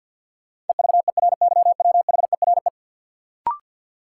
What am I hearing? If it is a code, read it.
Morse code, heard as a keyed tone, about 48 wpm, 700 Hz. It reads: EVERYWHERE
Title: Sentences from Top 1600 Words - New Word Drill Alt - 50wpm